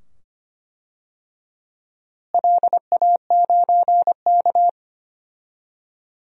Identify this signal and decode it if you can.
Morse code, heard as a keyed tone, about 25 wpm, 700 Hz. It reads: LA9K
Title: DXCC-100 Call Signs - 4 Characters - Repeated 2x - 25wpm